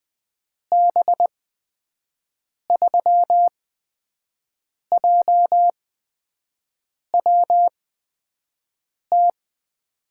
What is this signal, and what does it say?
Morse code, heard as a keyed tone, about 20 wpm, spaced as for 5 wpm, 700 Hz. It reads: B3JWT
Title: USA Call Signs - 6 Characters - 8x Character Spacing - 20wpm